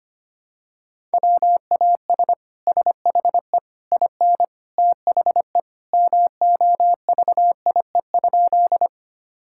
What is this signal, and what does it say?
Morse code, heard as a keyed tone, about 25 wpm, 700 Hz. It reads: WAS SHE IN THE MOVIE?
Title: Sentences from Top 1000 Words - Review Part 5 - 25wpm